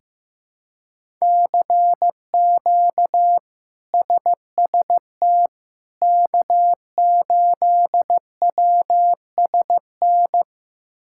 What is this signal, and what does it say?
Morse code, heard as a keyed tone, about 15 wpm, 700 Hz. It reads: CQ SST K8WSN
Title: K1USN SST QSO Practice - 15wpm